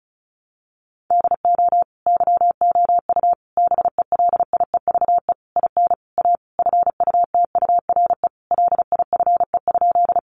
Morse code, heard as a keyed tone, about 34 wpm, 700 Hz. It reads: DO YOU BELIEVE IN A FUTURE LIFE?
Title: Sentences from Top 1000 Words 35wpm